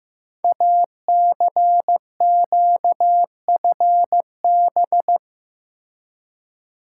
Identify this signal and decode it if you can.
Morse code, heard as a keyed tone, about 15 wpm, 700 Hz. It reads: ACQFB
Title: Code Groups - 5 Letters - 15wpm